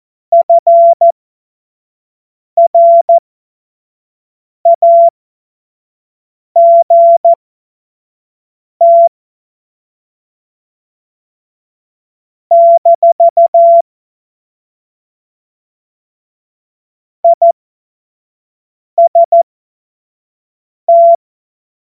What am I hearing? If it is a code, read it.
Morse code, heard as a keyed tone, about 14 wpm, spaced as for 5 wpm, 680 Hz. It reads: FRAGT - IST
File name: Morsekurs / Witze / Witz 12. Oktober 2021, 25 F70